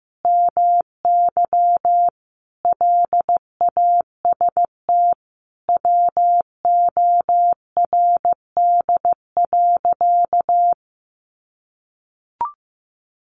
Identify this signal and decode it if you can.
Morse code, heard as a keyed tone, about 15 wpm, 700 Hz. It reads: MY LAST WORD.
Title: Sentences from Top 1200 Words 15wpm